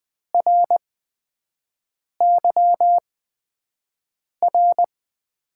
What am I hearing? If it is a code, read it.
Morse code, heard as a keyed tone, about 20 wpm, spaced as for 5 wpm, 700 Hz. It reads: RYR
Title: USA Call Signs - 5 Characters - 8x Character Spacing - 20wpm